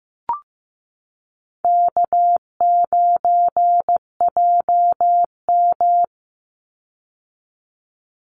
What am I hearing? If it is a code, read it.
Morse code, heard as a keyed tone, about 15 wpm, 700 Hz. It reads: K9JM
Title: Call Signs Repeated 2x - 15wpm